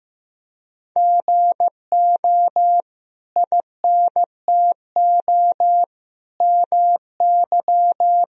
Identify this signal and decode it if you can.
Morse code, heard as a keyed tone, about 15 wpm, 700 Hz. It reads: GO INTO MY
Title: Sentences from Top 1000 Words - Review Part 1 - 15wpm